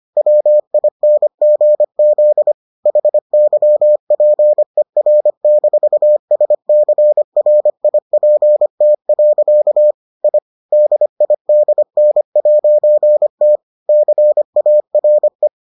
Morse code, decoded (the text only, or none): WINGZ HYPER-SCRIPT. I DIDN'T CARE